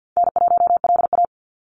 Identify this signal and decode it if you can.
Morse code, heard as a keyed tone, about 50 wpm, 700 Hz. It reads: N1LA
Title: USA Call Signs 50wpm